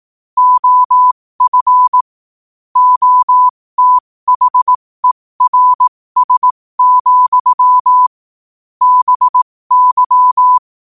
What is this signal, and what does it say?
Morse code, heard as a keyed tone, about 18 wpm, spaced as for 15 wpm, 1000 Hz. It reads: OF OTHERS, BY
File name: qotdfiles / 15wpm